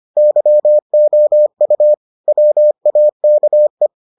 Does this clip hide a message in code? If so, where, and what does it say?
Yes: Morse code from 0.2 to 3.9 s, heard as a keyed tone, about 25 wpm, 600 Hz. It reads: YOU WAKE